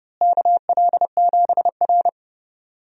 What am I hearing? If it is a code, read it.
Morse code, heard as a keyed tone, about 30 wpm, 700 Hz. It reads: KL7R
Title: DXCC-100 Call Signs - 4 Characters - Repeated 2x - 30wpm